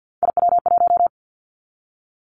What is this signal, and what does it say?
Morse code, heard as a keyed tone, about 50 wpm, 700 Hz. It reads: IW1
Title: Callsign Prefixes for Top 50 DXCC - 50wpm